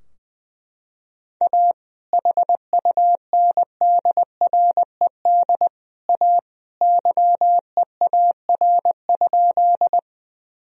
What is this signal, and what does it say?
Morse code, heard as a keyed tone, about 20 wpm, 700 Hz. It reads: A HUNDRED A YEAR?
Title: Sentences from Top 1000 Words - Review Part 2 - 20wpm